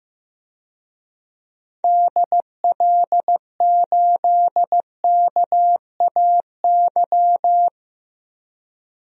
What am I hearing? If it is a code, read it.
Morse code, heard as a keyed tone, about 15 wpm, 700 Hz. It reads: DL8KAY
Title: DXCC-100 Call Signs - 6 Characters - 15wpm